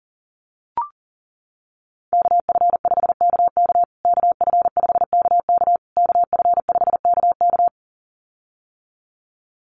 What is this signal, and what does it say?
Morse code, heard as a keyed tone, about 40 wpm, 700 Hz. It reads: KF5KK KF5KK KF5KK
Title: USA Call Signs Repeated 3x 40wpm